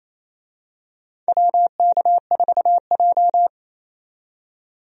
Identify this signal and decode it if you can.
Morse code, heard as a keyed tone, about 28 wpm, 690 Hz. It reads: WK4J
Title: USA Call Signs - 4 Characters - 28wpm